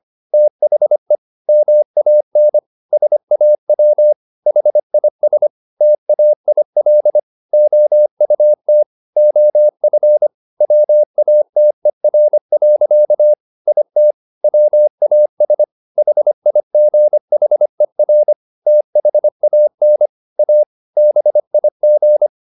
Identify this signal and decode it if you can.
Morse code, heard as a keyed tone, about 25 wpm, 600 Hz. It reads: THE MAN SAW HIS TAIL OUT OF WATER. IT WAS HIGHER THAN A BIG